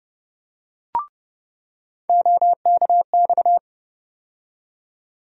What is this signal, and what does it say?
Morse code, heard as a keyed tone, about 30 wpm, 700 Hz. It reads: OKX
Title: USA Call Sign Postfixes - 3 Letters 30wpm